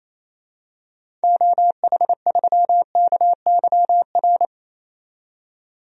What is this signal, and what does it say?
Morse code, heard as a keyed tone, about 28 wpm, 700 Hz. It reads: OH3KYR
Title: DXCC-100 Call Signs - 6 Characters - 28wpm